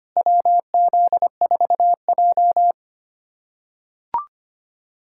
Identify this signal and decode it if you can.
Morse code, heard as a keyed tone, about 25 wpm, 700 Hz. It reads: WZ4J
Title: USA Callsign DE Callsign - 4 Characters - 25wpm